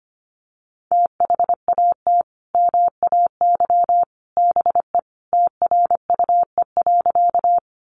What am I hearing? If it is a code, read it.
Morse code, heard as a keyed tone, about 25 wpm, 700 Hz. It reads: THAT MAY BE TRUE.
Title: Sentences from Top 400 Words 25wpm